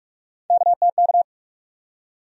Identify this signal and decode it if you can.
Morse code, heard as a keyed tone, about 46 wpm, 700 Hz. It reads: KTK
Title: USA Call Sign Postfixes - 3 Letters 45wpm